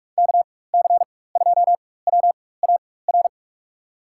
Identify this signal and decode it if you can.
Morse code, heard as a keyed tone, about 42 wpm, spaced as for 20 wpm, 700 Hz. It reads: KC2WAR
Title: USA Call Signs - 4x Character Spacing - 45wpm